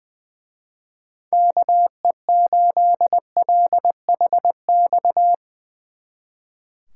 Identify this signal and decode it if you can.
Morse code, heard as a keyed tone, about 22 wpm, 700 Hz. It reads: KE8LHX
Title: Speed Racing - USA Call Signs 20wpm